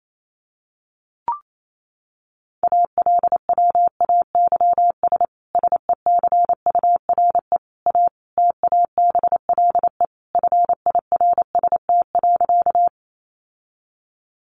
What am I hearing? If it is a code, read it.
Morse code, heard as a keyed tone, about 28 wpm, 700 Hz. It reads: ALWAYS SECURE A TABLE FIRST.
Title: Sentences from Top 1200 Words 28wpm